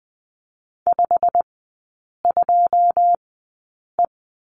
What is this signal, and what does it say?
Morse code, heard as a keyed tone, about 20 wpm, spaced as for 8 wpm, 700 Hz. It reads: 52E
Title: 152 Lesson 25e - Letter p - Three Characters 20wpm